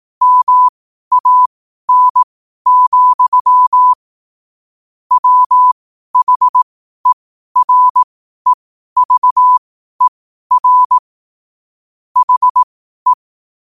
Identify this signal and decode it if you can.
Morse code, heard as a keyed tone, about 18 wpm, spaced as for 12 wpm, 1000 Hz. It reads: MAN, WHEREVER HE
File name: qotdfiles / 12wpm